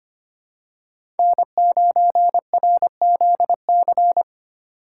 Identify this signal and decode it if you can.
Morse code, heard as a keyed tone, about 25 wpm, 700 Hz. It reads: N9RZC
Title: USA Call Signs - 5 Characters - 3x Character Spacing - 25wpm